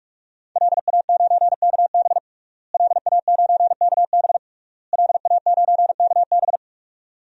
Morse code, heard as a keyed tone, about 46 wpm, 700 Hz. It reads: LA9KB LA9KB LA9KB
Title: DXCC-100 Call Signs - 5 Characters - Repeated 3x - 45wpm